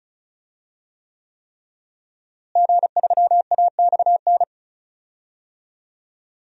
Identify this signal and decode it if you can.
Morse code, heard as a keyed tone, about 36 wpm, 700 Hz. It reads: G3AXN
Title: DXCC-100 Call Signs - 5 Characters - Repeated 3x - 35wpm